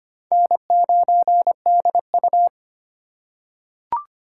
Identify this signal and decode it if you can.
Morse code, heard as a keyed tone, about 25 wpm, 700 Hz. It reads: N9DU
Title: USA Call Signs - 4 Characters - 4x Character Spacing - 25wpm